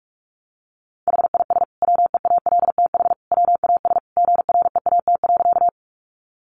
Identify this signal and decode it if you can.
Morse code, heard as a keyed tone, about 44 wpm, 700 Hz. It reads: HIS WEALTH WAS GREAT.
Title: Sentences from Top 1200 Words 45wpm